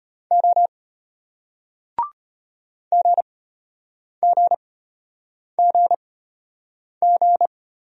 Morse code, heard as a keyed tone, about 33 wpm, spaced as for 8 wpm, 700 Hz. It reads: O GGGG